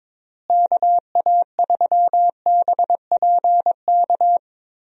Morse code, heard as a keyed tone, about 22 wpm, 700 Hz. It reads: KA3BPK
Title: USA Call Signs - 6 Characters - Repeated 3x - 22wpm